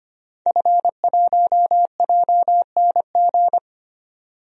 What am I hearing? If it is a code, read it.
Morse code, heard as a keyed tone, about 25 wpm, 700 Hz. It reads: F1JNG
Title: DXCC-100 Call Signs - 5 Characters - 25wpm